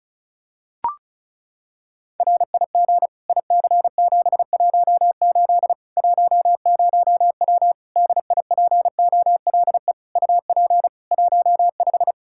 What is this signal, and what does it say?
Morse code, heard as a keyed tone, about 36 wpm, 700 Hz. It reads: RIG IC718 10W DIPOLE UP 15
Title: Practice QSO - Exchange 2 - Info - 35wpm